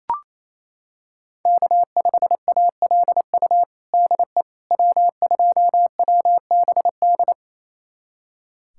Morse code, heard as a keyed tone, about 28 wpm, 700 Hz. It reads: K5ALU DE W2WBD
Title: USA Callsign DE Callsign - 5 Characters - 28wpm